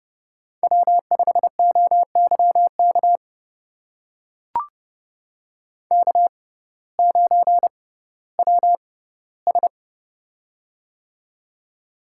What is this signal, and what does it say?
Morse code, heard as a keyed tone, about 30 wpm, spaced as for 16 wpm, 700 Hz. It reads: W5OYK K9WS
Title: USA Call Signs - 6x Character Spacing - 30wpm